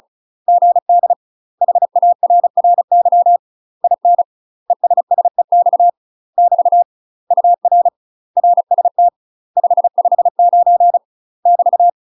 Morse code, heard as a keyed tone, about 36 wpm, 700 Hz. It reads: GD HARRY IN ESSEX <BT> UR RST 559 <BT>
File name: UK MASTER QSOs TRAINING / 35WPM / UK MASTER QSOs 0026